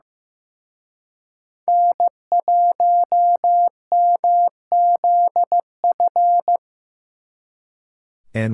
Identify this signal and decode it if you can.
Morse code, heard as a keyed tone, about 17 wpm, 700 Hz. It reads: N1MZF
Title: Speed Racing - USA 1x3 Call Signs 15wpm